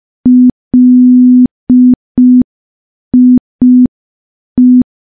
Morse code, heard as a keyed tone, about 5 wpm, 250 Hz. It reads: LIE
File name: morse / morse-training-05WPM